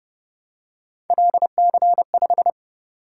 Morse code, heard as a keyed tone, about 29 wpm, 700 Hz. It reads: LC5